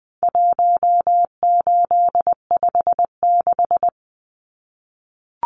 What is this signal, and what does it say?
Morse code, heard as a keyed tone, about 20 wpm, 700 Hz. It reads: 1856
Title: Numbers - 4-Digit 20wpm